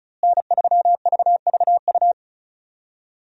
Morse code, heard as a keyed tone, about 36 wpm, 700 Hz. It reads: N3VVU
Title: USA Call Signs - 5 Characters - Repeated 3x - 35wpm